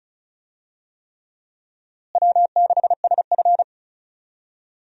Morse code, heard as a keyed tone, about 36 wpm, 700 Hz. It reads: W6SF
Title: USA Call Signs - 4 Characters - Repeated 3x - 35wpm